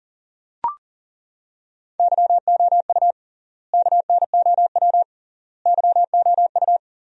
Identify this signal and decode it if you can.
Morse code, heard as a keyed tone, about 40 wpm, 700 Hz. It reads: YOU KNOW YOU
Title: 3x Word-Spacing - Sentences from Top 500 Words - 40wpm